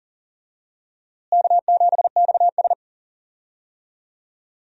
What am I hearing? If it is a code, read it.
Morse code, heard as a keyed tone, about 40 wpm, 700 Hz. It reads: K7XS
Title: USA Call Signs - 4 Characters - Repeated 2x - 40wpm